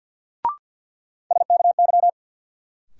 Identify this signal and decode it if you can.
Morse code, heard as a keyed tone, about 47 wpm, 700 Hz. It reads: SKY
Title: Sentences from Top 1200 Words - New Word Drill Alt - 50wpm